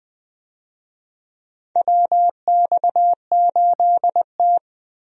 Morse code, heard as a keyed tone, about 21 wpm, 690 Hz. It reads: WX8T
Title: USA Call Signs - 4 Characters - Repeated 3x - 20wpm